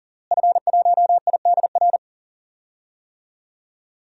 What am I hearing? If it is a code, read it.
Morse code, heard as a keyed tone, about 40 wpm, 700 Hz. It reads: F1IDR
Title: DXCC-100 Call Signs - 5 Characters - Repeated 3x - 40wpm